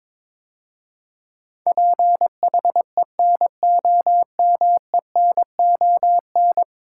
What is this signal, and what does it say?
Morse code, heard as a keyed tone, about 22 wpm, 700 Hz. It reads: PHENOMENON
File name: practice / 22wpm / 10 Letter Words - First 3 Letters Given - 22wpm